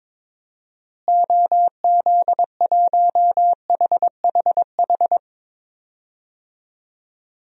Morse code, heard as a keyed tone, about 21 wpm, 700 Hz. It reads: OZ1HHH
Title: DXCC-100 Call Signs - 6 Characters - Repeated 2x - 22wpm